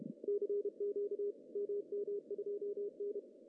Morse code, heard as a keyed tone, about 30 wpm, 410 Hz. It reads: CQ MM2N
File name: CW / cqww-cw-k5tr